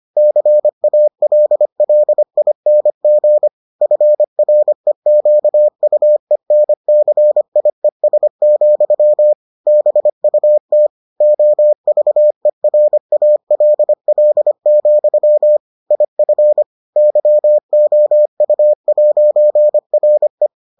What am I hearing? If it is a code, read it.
Morse code, heard as a keyed tone, about 25 wpm, 600 Hz. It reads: CALLING FREQUENCIES, BUT OVERALL, IF YOU'RE